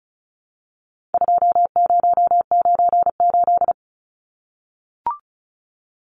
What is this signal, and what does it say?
Morse code, heard as a keyed tone, about 34 wpm, 700 Hz. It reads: 2098
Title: Numbers - 4-Digit 35wpm